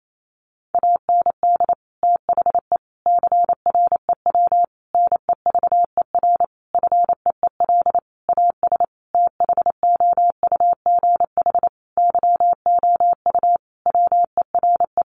Morse code, heard as a keyed tone, about 28 wpm, 700 Hz. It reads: AND THE CREW NEVER FEEL AS THOUGH YOU WERE
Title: Sentences from Top 1500 Words 28wpm